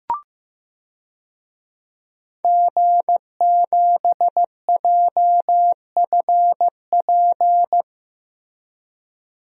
Morse code, heard as a keyed tone, about 15 wpm, 700 Hz. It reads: G7JFP
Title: DXCC-100 Call Signs - 5 Characters - Repeated 2x - 15wpm